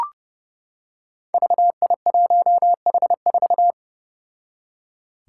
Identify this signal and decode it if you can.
Morse code, heard as a keyed tone, about 31 wpm, 700 Hz. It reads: VI1H4
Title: Five Challenging Letter-Number Combinations 30wpm